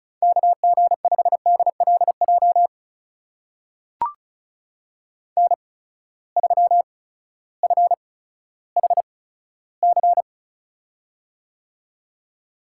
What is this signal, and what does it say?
Morse code, heard as a keyed tone, about 36 wpm, spaced as for 16 wpm, 700 Hz. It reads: KG5DLJ N3FHC
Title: USA Call Signs - 8x Character Spacing - 35wpm